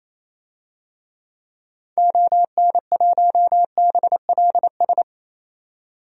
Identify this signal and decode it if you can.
Morse code, heard as a keyed tone, about 28 wpm, 700 Hz. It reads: ON1BLS
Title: DXCC-100 Call Signs - 6 Characters - 28wpm